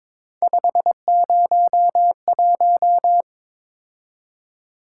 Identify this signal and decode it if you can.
Morse code, heard as a keyed tone, about 22 wpm, 700 Hz. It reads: 501
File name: practice / 22wpm / Three Letter-Number Combinations - 22wpm